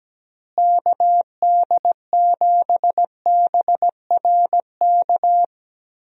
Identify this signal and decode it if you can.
Morse code, heard as a keyed tone, about 17 wpm, 700 Hz. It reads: KD7BRK
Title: USA Call Signs - 6 Characters - Repeated 3x - 17wpm